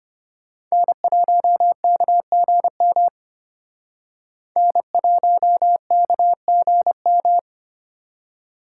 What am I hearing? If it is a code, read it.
Morse code, heard as a keyed tone, about 28 wpm, 700 Hz. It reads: N1KGM N1KGM
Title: Speed Racing - USA 1x3 Call Signs 25wpm